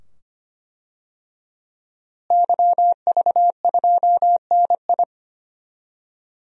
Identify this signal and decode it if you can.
Morse code, heard as a keyed tone, about 25 wpm, 700 Hz. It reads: YV2NI
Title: DXCC-100 Call Signs - 5 Characters - Repeated 2x - 25wpm